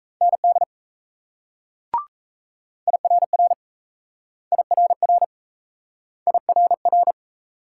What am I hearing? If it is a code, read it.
Morse code, heard as a keyed tone, about 37 wpm, 700 Hz. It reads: ND IRR IRR IRR